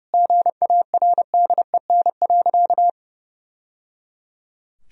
Morse code, heard as a keyed tone, about 30 wpm, 700 Hz. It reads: GARDEN.
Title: Sentences from Top 1000 Words - Review Part 4 - 30wpm